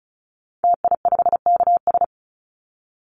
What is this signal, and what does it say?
Morse code, heard as a keyed tone, about 34 wpm, 700 Hz. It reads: TI5KS
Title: Call Signs Repeated 3x - 35wpm